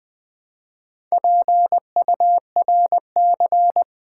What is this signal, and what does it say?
Morse code, heard as a keyed tone, about 20 wpm, 700 Hz. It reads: PURC
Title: Build A Word - 4 Letters At A Time - 8-Letter Words - 20wpm